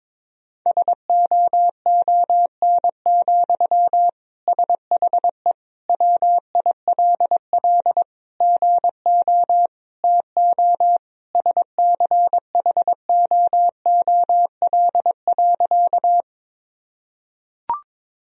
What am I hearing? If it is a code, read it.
Morse code, heard as a keyed tone, about 22 wpm, 700 Hz. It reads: SOON, SHE WILL GO TO SCHOOL.
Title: Sentences from Top 1000 Words - Review Part 1 - 22wpm